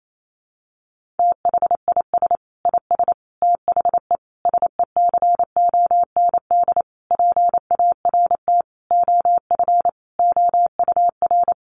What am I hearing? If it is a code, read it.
Morse code, heard as a keyed tone, about 28 wpm, 700 Hz. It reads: THIS IS THE SECOND PART OF OUR
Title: Sentences from Top 1400 Words 28wpm